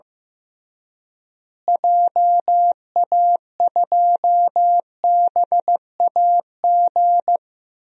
Morse code, heard as a keyed tone, about 15 wpm, 700 Hz. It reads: JA2BAG
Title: DXCC-100 Call Signs - 6 Characters - Repeated 3x - 15wpm